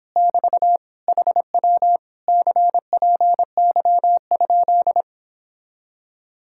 Morse code, heard as a keyed tone, about 26 wpm, 700 Hz. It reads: <BT> HW CPY?
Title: Speed-Racing Common QSO Phrases Encore - 22wpm